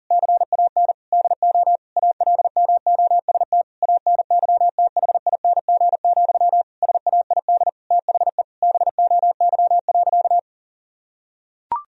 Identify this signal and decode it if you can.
Morse code, heard as a keyed tone, about 37 wpm, 700 Hz. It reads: CAN DO ALMOST ANYTHING, SAID THE BOY.
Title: Sentences from Top 1000 Words - Review Part 4 - 40wpm